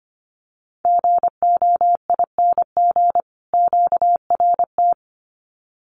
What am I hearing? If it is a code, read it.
Morse code, heard as a keyed tone, about 25 wpm, 700 Hz. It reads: GOING QRT
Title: Common QSO Phrases 25wpm